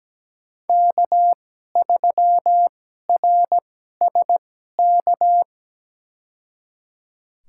Morse code, heard as a keyed tone, about 17 wpm, spaced as for 12 wpm, 700 Hz. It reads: K3RSK
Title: USA Call Signs - 5 Characters - 2x Character Spacing - 17wpm